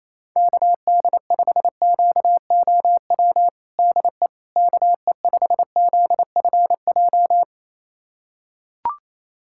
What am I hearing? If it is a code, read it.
Morse code, heard as a keyed tone, about 28 wpm, 700 Hz. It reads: KD5QOW DE KE5ZFJ
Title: USA Callsign DE Callsign 28wpm